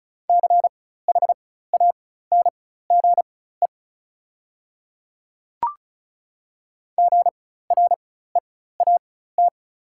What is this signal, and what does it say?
Morse code, heard as a keyed tone, about 34 wpm, spaced as for 16 wpm, 700 Hz. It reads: CHANGE GREAT
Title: 4x Character Spacing - Top 200 Words - 35wpm